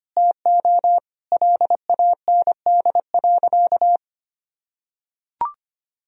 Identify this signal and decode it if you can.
Morse code, heard as a keyed tone, about 25 wpm, 700 Hz. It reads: TO LAND.
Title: Sentences from Top 1000 Words - Review Part 1 - 25wpm